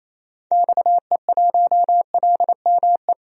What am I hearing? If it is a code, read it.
Morse code, heard as a keyed tone, about 28 wpm, 700 Hz. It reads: XE1LME
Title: DXCC-100 Call Signs - 6 Characters - 28wpm